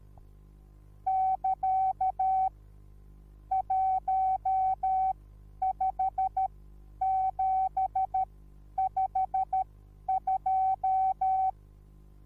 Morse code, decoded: <KA> 15752